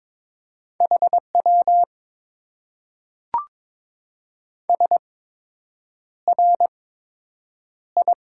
Morse code, heard as a keyed tone, about 22 wpm, spaced as for 8 wpm, 700 Hz. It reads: HW SRI
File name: Top 90 QSO Elements - 8x Character Spacing - 22wpm